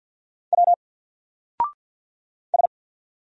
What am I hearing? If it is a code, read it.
Morse code, heard as a keyed tone, about 50 wpm, 700 Hz. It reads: W S
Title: Single Letter-Number 50wpm